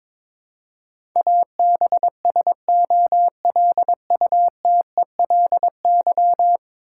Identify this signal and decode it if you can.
Morse code, heard as a keyed tone, about 22 wpm, 700 Hz. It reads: ABSOLUTELY
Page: morse-code-ninja-practice-22wpm